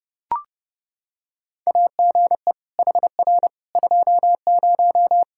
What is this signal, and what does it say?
Morse code, heard as a keyed tone, about 32 wpm, 700 Hz. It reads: AGE HR 20
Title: Practice QSO - Exchange 3 - More Info - 30wpm